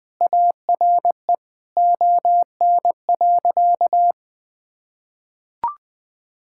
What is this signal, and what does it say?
Morse code, heard as a keyed tone, about 20 wpm, 700 Hz. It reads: ARE ON.
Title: Sentences with 2-Word Phrases - Spoken Beginning - 20wpm